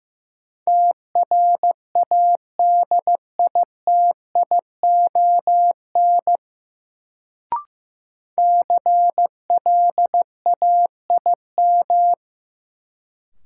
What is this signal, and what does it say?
Morse code, heard as a keyed tone, about 15 wpm, 700 Hz. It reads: TRADITION CLAIM
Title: Sentences from Top 1500 Words - New Word Drill - 15wpm